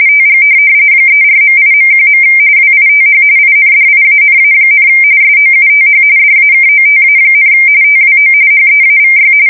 RTTY, about 45.45 baud. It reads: TG 34.8.1. WITH INDIANA, SOUTH DAKOTA, TWO HEAVY
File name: fsk170 / USS-Massachusetts